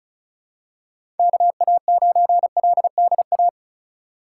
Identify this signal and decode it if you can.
Morse code, heard as a keyed tone, about 36 wpm, 700 Hz. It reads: KA9LDA
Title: USA Call Signs - 6 Characters - 3x Character Spacing - 35wpm